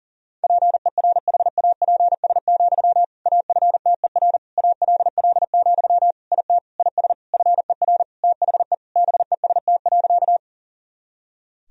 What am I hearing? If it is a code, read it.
Morse code, heard as a keyed tone, about 40 wpm, 700 Hz. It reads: PERHAPS, AFTER ALL, IT IS FER THE BEST.
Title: Sentences from Top 500 Words and Abbreviations 40wpm